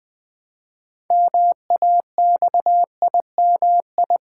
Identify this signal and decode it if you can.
Morse code, heard as a keyed tone, about 20 wpm, 700 Hz. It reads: MAXIMI
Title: Build A Word - 3 Letters At A Time - 8-Letter Words - 20wpm